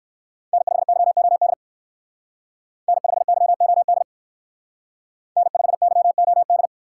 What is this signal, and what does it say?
Morse code, heard as a keyed tone, about 67 wpm, 700 Hz. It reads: N5YOD N5YOD N5YOD